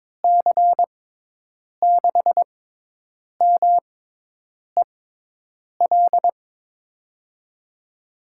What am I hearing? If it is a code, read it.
Morse code, heard as a keyed tone, about 23 wpm, spaced as for 8 wpm, 700 Hz. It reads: C6MEL
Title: USA Call Signs - 6 Characters - 6x Character Spacing - 22wpm